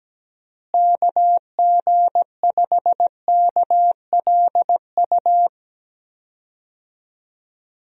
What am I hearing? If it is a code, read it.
Morse code, heard as a keyed tone, about 17 wpm, 700 Hz. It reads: KG5KLU
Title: USA Call Signs - 6 Characters - Repeated 2x - 17wpm